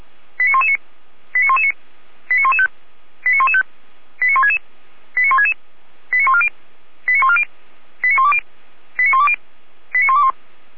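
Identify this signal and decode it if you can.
Five-tone selective call (ZVEI), heard as quick runs of five tones, tones of about 70 ms. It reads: 89119 89119 89115 89115 89166 89166 89140 89140 89120 89120 89121
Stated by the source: ZVEI five-tone selective call 89080 received on an SDR.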